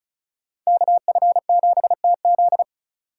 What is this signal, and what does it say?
Morse code, heard as a keyed tone, about 42 wpm, 700 Hz. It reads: KF7TZ
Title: Speed Racing - USA 2x2 Call Signs 35wpm